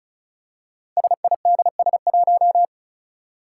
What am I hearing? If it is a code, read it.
Morse code, heard as a keyed tone, about 35 wpm, 700 Hz. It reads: SIDS1
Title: Five Challenging Letter-Number Combinations 35wpm